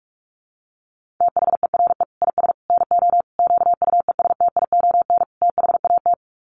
Morse code, heard as a keyed tone, about 44 wpm, 700 Hz. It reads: THERE IS NO QUESTION THAT
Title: Sets of 5 Words Encore 45wpm